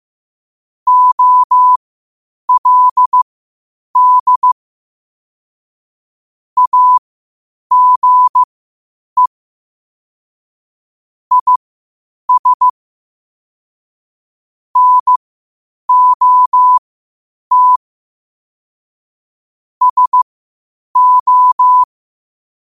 Morse code, heard as a keyed tone, about 15 wpm, spaced as for 8 wpm, 1000 Hz. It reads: OLD AGE IS NOT SO
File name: qotdfiles / 8wpm